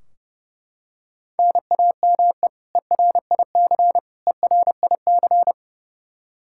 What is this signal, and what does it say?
Morse code, heard as a keyed tone, about 32 wpm, 700 Hz. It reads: NAME ERIC ERIC
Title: Practice QSO - Exchange 1 - Beginning - 30wpm